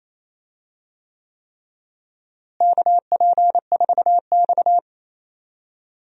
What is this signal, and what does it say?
Morse code, heard as a keyed tone, about 28 wpm, 700 Hz. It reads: KP4X
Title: DXCC-100 Call Signs - 4 Characters - Repeated 3x - 28wpm